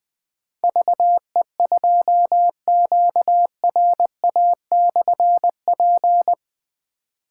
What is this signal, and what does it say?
Morse code, heard as a keyed tone, about 20 wpm, 700 Hz. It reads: VE2QRA/P
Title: Difficult Call Signs 20wpm